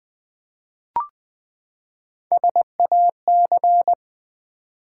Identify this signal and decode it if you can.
Morse code, heard as a keyed tone, about 20 wpm, 700 Hz. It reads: SAC
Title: Build A Word - 3 Letters At A Time - 6-Letter Words - 20wpm